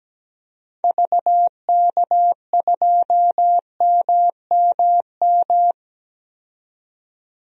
Morse code, heard as a keyed tone, about 17 wpm, 700 Hz. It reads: VK2MMM
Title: DXCC-100 Call Signs - 6 Characters - 17wpm